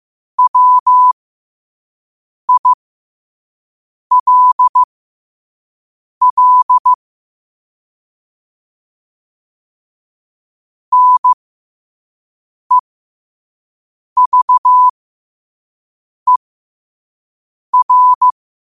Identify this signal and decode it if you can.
Morse code, heard as a keyed tone, about 15 wpm, spaced as for 5 wpm, 1000 Hz. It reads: WILL NEVER
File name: qotdfiles / 5wpm